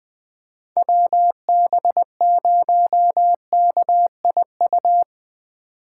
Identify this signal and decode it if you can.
Morse code, heard as a keyed tone, about 20 wpm, 700 Hz. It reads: WB0KIU
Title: 2023 SOTA Chaser Call Signs - 20wpm